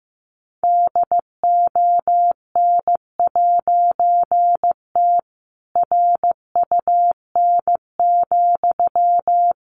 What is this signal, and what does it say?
Morse code, heard as a keyed tone, about 15 wpm, 700 Hz. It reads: DON'T RUN,
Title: A Sound Of Thunder 15wpm